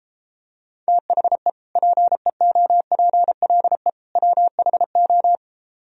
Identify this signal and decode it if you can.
Morse code, heard as a keyed tone, about 33 wpm, 700 Hz. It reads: THE PEOPLE WHO